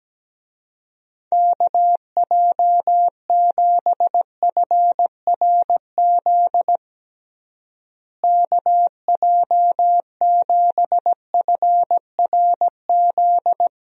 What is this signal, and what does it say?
Morse code, heard as a keyed tone, about 17 wpm, 700 Hz. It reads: KJ7FRZ KJ7FRZ
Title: USA Call Signs - 6 Characters - Repeated 3x - 17wpm